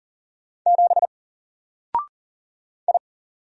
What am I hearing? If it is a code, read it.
Morse code, heard as a keyed tone, about 40 wpm, 700 Hz. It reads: 7 I
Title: Single Letter-Number 40wpm